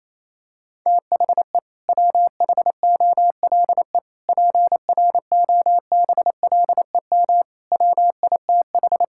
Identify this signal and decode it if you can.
Morse code, heard as a keyed tone, about 28 wpm, 700 Hz. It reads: THE WHOLE PROBLEM WITH
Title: Sentences from Top 1000 Words - Review Part 6 - 28wpm